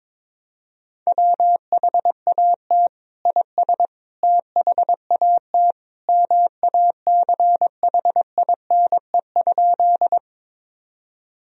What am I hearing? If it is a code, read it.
Morse code, heard as a keyed tone, about 22 wpm, 700 Hz. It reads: WHAT IS THAT MACHINE?
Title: Sentences from Top 1000 Words - Review Part 2 - 22wpm